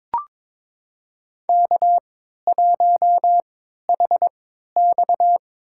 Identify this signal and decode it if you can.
Morse code, heard as a keyed tone, about 22 wpm, spaced as for 13 wpm, 700 Hz. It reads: K1HX
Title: USA Call Signs - 4 Characters - 3x Character Spacing - 22wpm